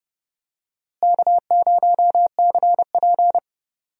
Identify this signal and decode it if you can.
Morse code, heard as a keyed tone, about 31 wpm, 700 Hz. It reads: K0CP
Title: USA Call Signs - 4 Characters - Repeated 3x - 30wpm